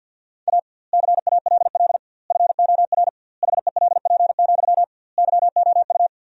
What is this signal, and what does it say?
Morse code, heard as a keyed tone, about 49 wpm, 700 Hz. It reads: A CALL FOR HELP, YOU
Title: 2x Word-Spacing - Sentences from Top 300 Words - 50wpm